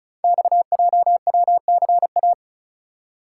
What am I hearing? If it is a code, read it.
Morse code, heard as a keyed tone, about 35 wpm, 700 Hz. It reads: XJWCA
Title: Code Groups - 5 Letter:Numbers - 35wpm